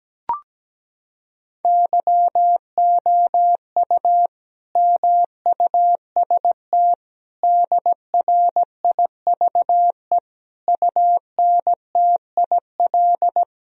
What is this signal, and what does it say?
Morse code, heard as a keyed tone, about 17 wpm, 700 Hz. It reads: YOU MUST DRIVE UNTIL
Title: Sentences from Top 1000 Words - Review Part 2 - 17wpm